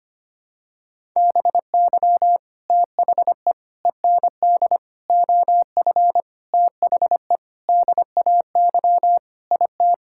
Morse code, heard as a keyed tone, about 25 wpm, 700 Hz. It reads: BY THE END OF THE DAY IT
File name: practice / 25wpm / Sentences with 4-Word Phrases - 25wpm